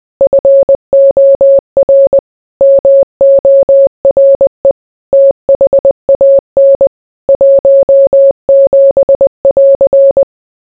Morse code, heard as a keyed tone, about 20 wpm, 550 Hz. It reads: FOR MORE THAN 17<AR>